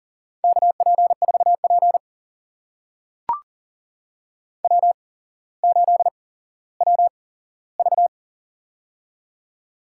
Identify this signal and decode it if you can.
Morse code, heard as a keyed tone, about 39 wpm, spaced as for 17 wpm, 700 Hz. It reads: KP4P W8WV
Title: USA Call Signs - 4 Characters - 8x Character Spacing - 40wpm